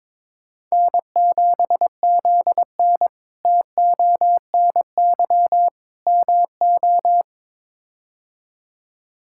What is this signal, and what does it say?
Morse code, heard as a keyed tone, about 22 wpm, 700 Hz. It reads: N7ZN TONY MO
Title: K1USN SST QSO Practice - 22wpm